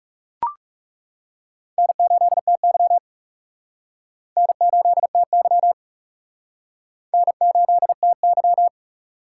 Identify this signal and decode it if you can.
Morse code, heard as a keyed tone, about 40 wpm, 700 Hz. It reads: N8TY N8TY N8TY